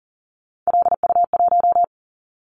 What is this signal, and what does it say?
Morse code, heard as a keyed tone, about 42 wpm, 700 Hz. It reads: LU1
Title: Callsign Prefixes for Top 100 DXCC - 40wpm